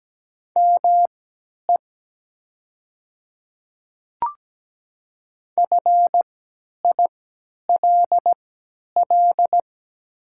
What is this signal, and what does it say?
Morse code, heard as a keyed tone, about 17 wpm, spaced as for 10 wpm, 700 Hz. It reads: ME FILL
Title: 3x Character Spacing - Top 500 Words - 17wpm